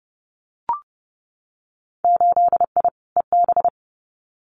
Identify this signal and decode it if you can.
Morse code, heard as a keyed tone, about 30 wpm, 700 Hz. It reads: 8I EB
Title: Winter Field Day Exchanges Encore 30wpm